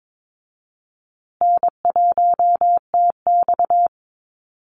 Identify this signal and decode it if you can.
Morse code, heard as a keyed tone, about 22 wpm, 700 Hz. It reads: N1TX
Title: Easy Call Signs Repeated 2x - 22wpm